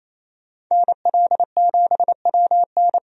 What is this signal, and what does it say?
Morse code, heard as a keyed tone, about 28 wpm, 700 Hz. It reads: NL7WN
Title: USA Call Signs - 5 Characters - 28wpm